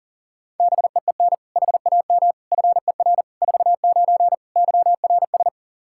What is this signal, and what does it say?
Morse code, heard as a keyed tone, about 40 wpm, 700 Hz. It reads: BEEN HAM FER 49 YRS
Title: Practice QSO - Exchange 3 - More Info - 40wpm